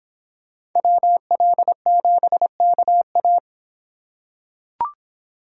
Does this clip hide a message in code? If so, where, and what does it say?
Yes: Morse code from 0.8 to 3.4 s, heard as a keyed tone, about 25 wpm, 700 Hz. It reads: WL7KA